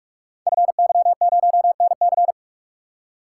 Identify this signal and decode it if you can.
Morse code, heard as a keyed tone, about 44 wpm, 700 Hz. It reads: FY0NC
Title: Code Groups - 5 Letter:Numbers - 45wpm